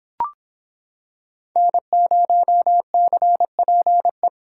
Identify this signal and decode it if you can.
Morse code, heard as a keyed tone, about 25 wpm, 700 Hz. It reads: N0CPE